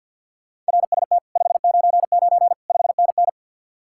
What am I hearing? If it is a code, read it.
Morse code, heard as a keyed tone, about 65 wpm, 700 Hz. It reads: RST 599 5NN